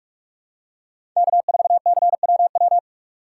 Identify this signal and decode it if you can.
Morse code, heard as a keyed tone, about 46 wpm, 700 Hz. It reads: K4CWW
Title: USA Call Signs - 6x Character Spacing - 45wpm